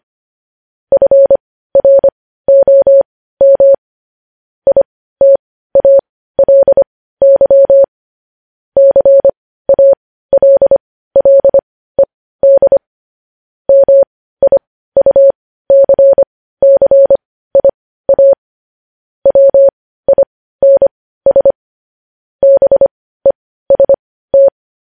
Morse code, heard as a keyed tone, about 25 wpm, spaced as for 15 wpm, 550 Hz. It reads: FROM ITALY CALLED MIUCCIA WINS BEST
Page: news headlines in morse code at 15 wpm